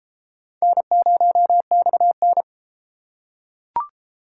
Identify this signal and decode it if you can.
Morse code, heard as a keyed tone, about 30 wpm, 700 Hz. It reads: N0XN